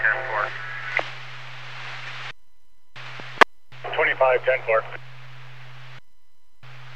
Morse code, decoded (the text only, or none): none